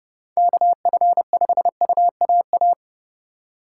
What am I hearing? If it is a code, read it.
Morse code, heard as a keyed tone, about 28 wpm, 700 Hz. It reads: KF5UAA